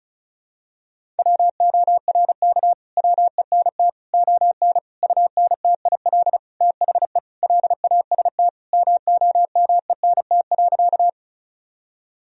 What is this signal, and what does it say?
Morse code, heard as a keyed tone, about 36 wpm, 700 Hz. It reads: WORK WENT ON UNTIL THE LAST MOMENT.
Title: Sentences from Top 1000 Words - Review Part 1 - 35wpm